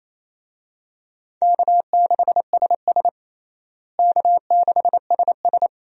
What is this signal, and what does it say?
Morse code, heard as a keyed tone, about 28 wpm, 700 Hz. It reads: K6SS K6SS
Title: USA Call Signs - 4 Characters - Repeated 2x - 28wpm